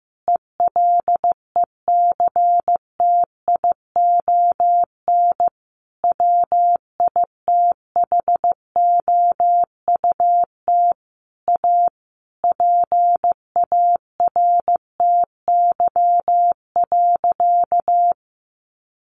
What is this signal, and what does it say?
Morse code, heard as a keyed tone, about 15 wpm, 700 Hz. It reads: ELECTION WITHOUT A PARTY.
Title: Sentences from Top 1500 Words 15wpm